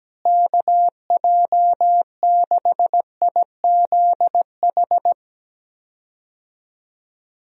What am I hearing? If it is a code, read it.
Morse code, heard as a keyed tone, about 17 wpm, 700 Hz. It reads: KJ6IZH
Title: USA Call Signs - 6 Characters - Repeated 2x - 17wpm